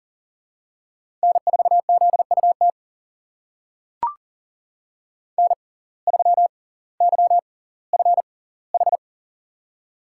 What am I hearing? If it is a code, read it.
Morse code, heard as a keyed tone, about 39 wpm, spaced as for 20 wpm, 700 Hz. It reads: N4ZUT N3YFH
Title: USA Call Signs - 5 Characters - 6x Character Spacing - 40wpm